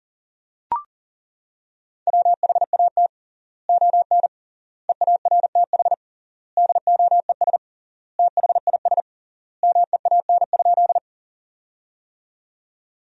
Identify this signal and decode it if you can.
Morse code, heard as a keyed tone, about 40 wpm, 700 Hz. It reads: WHAT ON EARTH DOES THIS MEAN?
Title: 3x Word-Spacing - Sentences from Top 200 Words - 40wpm